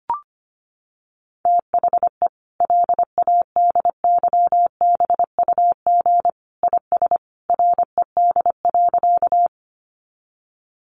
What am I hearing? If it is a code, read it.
Morse code, heard as a keyed tone, about 25 wpm, 700 Hz. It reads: THE LADYBUG IS RED.
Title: Additional Easy Phrases 25wpm